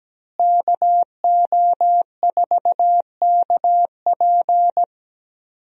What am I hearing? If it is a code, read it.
Morse code, heard as a keyed tone, about 17 wpm, 700 Hz. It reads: KO4KP
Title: USA Call Signs - 5 Characters - Repeated 2x - 17wpm